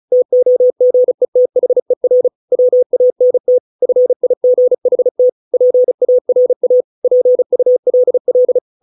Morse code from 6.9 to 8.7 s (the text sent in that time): PULL